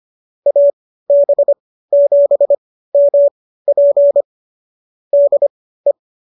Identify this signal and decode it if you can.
Morse code, heard as a keyed tone, about 25 wpm, spaced as for 15 wpm, 570 Hz. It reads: AB7MP DE